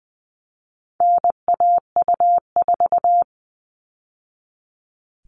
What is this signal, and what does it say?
Morse code, heard as a keyed tone, about 20 wpm, 700 Hz. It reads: NAU4
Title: All Cut Numbers - 4-Digit 20wpm